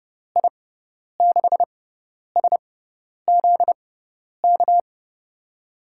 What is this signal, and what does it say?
Morse code, heard as a keyed tone, about 30 wpm, spaced as for 10 wpm, 700 Hz. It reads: I6SZK
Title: USA Call Signs - 6 Characters - 6x Character Spacing - 30wpm